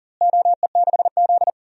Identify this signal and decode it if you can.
Morse code, heard as a keyed tone, about 40 wpm, 700 Hz. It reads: OE6Z
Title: DXCC-100 Call Signs - 4 Characters - Repeated 3x - 40wpm